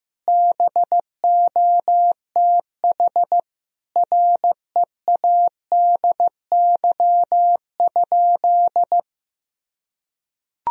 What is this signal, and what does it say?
Morse code, heard as a keyed tone, about 15 wpm, 700 Hz. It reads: BOTH READY?
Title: Sentences from Top 1000 Words - Review Part 1 - 15wpm